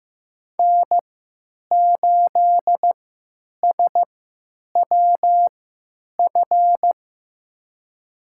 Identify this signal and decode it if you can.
Morse code, heard as a keyed tone, about 15 wpm, spaced as for 9 wpm, 700 Hz. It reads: N8SWF
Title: USA Call Signs - 5 Characters - 3x Character Spacing - 15wpm